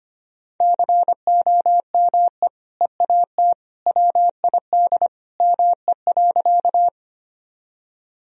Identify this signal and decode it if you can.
Morse code, heard as a keyed tone, about 25 wpm, 700 Hz. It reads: COME EAT WID ME.
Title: Sentences from Top 300 Words and Abbreviations 25wpm